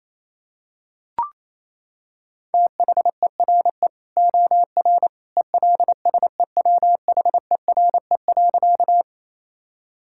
Morse code, heard as a keyed tone, about 28 wpm, 700 Hz. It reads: THERE OR ELSEWHERE.
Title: Sentences from Top 1600 Words 28wpm